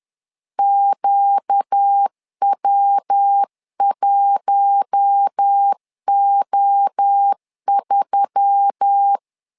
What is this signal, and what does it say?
Morse code, heard as a keyed tone, about 10 wpm, 800 Hz. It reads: QW1O3
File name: LEZIONI / 4 Q1S3 10wpm 2427